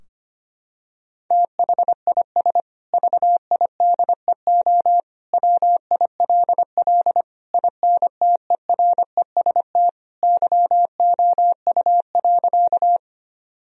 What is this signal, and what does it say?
Morse code, heard as a keyed tone, about 25 wpm, 700 Hz. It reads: THIS VIDEO WILL INTEREST YOU.
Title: Sentences from Top 1000 Words - Review Part 4 - 25wpm